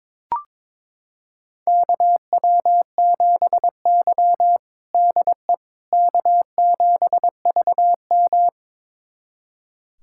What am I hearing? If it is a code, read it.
Morse code, heard as a keyed tone, about 22 wpm, 700 Hz. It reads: KW7Y DE K7VM
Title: USA Callsign DE Callsign - 4 Characters - 22wpm